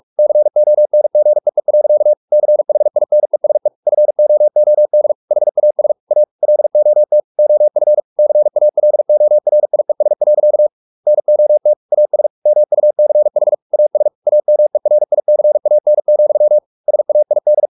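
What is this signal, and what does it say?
Morse code, heard as a keyed tone, about 44 wpm, 600 Hz. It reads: CONGEE. CHINESE FOOD HAS A LOT OF CALORIES. NOT AS MUCH AS AMERICAN, SAID
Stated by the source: Morse practice at 45 wpm, the novel Never by Ken Follett in English, from CW med Gnister.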